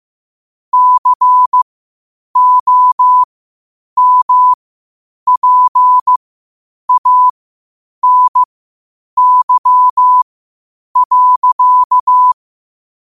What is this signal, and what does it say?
Morse code, heard as a keyed tone, about 15 wpm, spaced as for 8 wpm, 1000 Hz. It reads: COMPANY.
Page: qotd-at-8-wpm